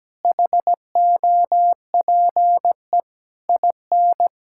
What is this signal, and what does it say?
Morse code, heard as a keyed tone, about 17 wpm, 700 Hz. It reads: HOPE IN
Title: Sentences from Top 1000 Words - Review Part 6 - 17wpm